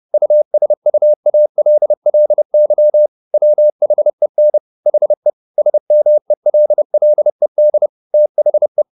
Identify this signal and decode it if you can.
Morse code, heard as a keyed tone, about 30 wpm, 600 Hz. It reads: USUALLY WHEN HE SMELLED THE
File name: the old man and the sea / the old man and the sea 1 30wpm